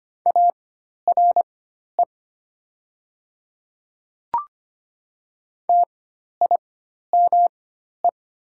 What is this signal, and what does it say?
Morse code, heard as a keyed tone, about 25 wpm, spaced as for 12 wpm, 700 Hz. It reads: ARE TIME
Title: 4x Character Spacing - Top 100 Words - 25wpm